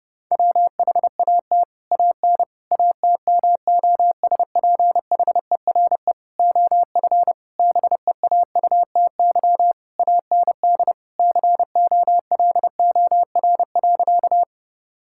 Morse code, heard as a keyed tone, about 30 wpm, 700 Hz. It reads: WHAT AN ATMOSPHERE OF BEAUTY AND COLOR.
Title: Sentences from Top 1600 Words 30wpm